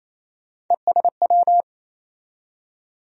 Morse code, heard as a keyed tone, about 28 wpm, 700 Hz. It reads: ESW
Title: USA Call Sign Postfixes - 3 Letters 28wpm